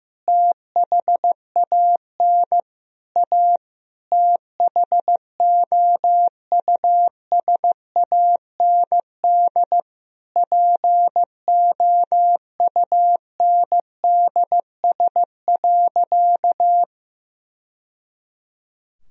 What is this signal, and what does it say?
Morse code, heard as a keyed tone, about 16 wpm, 700 Hz. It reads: THAN A THOUSAND POUNDS.
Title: Sentences from Top 1000 Words - Review Part 3 - 15wpm